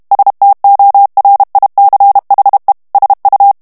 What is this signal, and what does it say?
Morse code, heard as a keyed tone, about 32 wpm, 800 Hz. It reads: STORICHE SU
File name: morse mp3 / ITALIEN texte clair 1 vitesse 30 mots minute